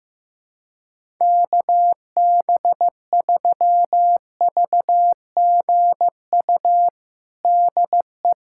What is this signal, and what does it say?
Morse code, heard as a keyed tone, about 15 wpm, 700 Hz. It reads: KB3VGU DE
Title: USA Callsign DE Callsign 15wpm